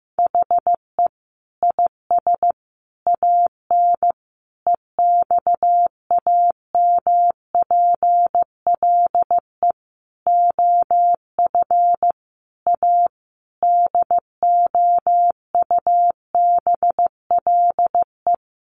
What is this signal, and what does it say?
Morse code, heard as a keyed tone, about 15 wpm, 700 Hz. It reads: HE IS AN EXAMPLE OF A DOUBLE
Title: Sentences from Top 1500 Words 15wpm